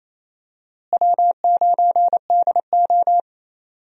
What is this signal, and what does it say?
Morse code, heard as a keyed tone, about 28 wpm, 700 Hz. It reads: W9DO
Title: USA Call Signs - 4 Characters - Repeated 3x - 28wpm